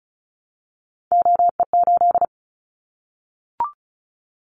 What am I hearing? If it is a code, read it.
Morse code, heard as a keyed tone, about 34 wpm, 700 Hz. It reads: OE8
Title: Callsign Prefixes for Top 100 DXCC - 35wpm